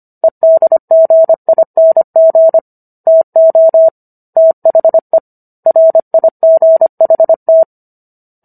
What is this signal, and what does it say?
Morse code, heard as a keyed tone, about 25 wpm, 650 Hz. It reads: EDGING TO THE RIGHT